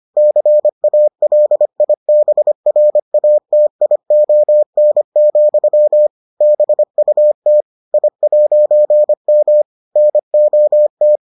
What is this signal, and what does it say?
Morse code, heard as a keyed tone, about 25 wpm, 600 Hz. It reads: CALIBRATION, BUT I'M NOT